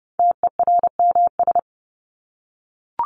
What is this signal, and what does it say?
Morse code, heard as a keyed tone, about 30 wpm, 700 Hz. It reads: TERMS
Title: ES:S Suffix - Top 100 Words with Base Words 30wpm